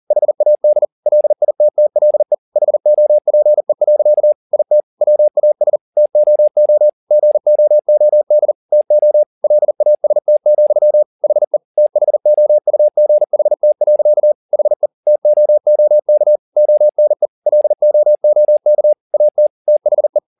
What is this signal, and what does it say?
Morse code, heard as a keyed tone, about 40 wpm, 600 Hz. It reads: HAD LITTLE HOPE. IT WAS TOO GOOD TO LAST, HE THOUGHT. HE TOOK ONE LOOK AT THE
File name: the old man and the sea 2 40wpm